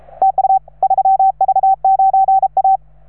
Morse code, heard as a keyed tone, about 35 wpm, 780 Hz. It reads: TU 3V9A